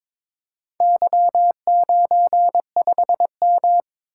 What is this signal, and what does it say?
Morse code, heard as a keyed tone, about 22 wpm, 700 Hz. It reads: Y95M
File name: practice / 22wpm / Four Letter-Number Combinations - 22wpm